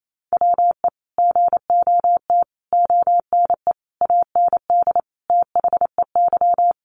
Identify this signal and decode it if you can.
Morse code, heard as a keyed tone, about 28 wpm, 700 Hz. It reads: WE GOT ONE AND THEY
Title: Sentences from Top 300 Words 28wpm